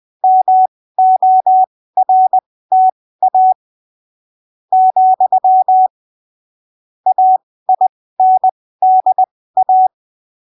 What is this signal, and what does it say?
Morse code, heard as a keyed tone, about 20 wpm, spaced as for 15 wpm, 750 Hz. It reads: MORTA , AINDA
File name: arquivos / texto / texto 20 20ppm